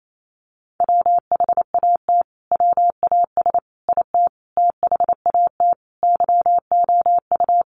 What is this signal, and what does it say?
Morse code, heard as a keyed tone, about 28 wpm, 700 Hz. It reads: WHAT WAS IT THAT YOU
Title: Sentences from Top 300 Words 28wpm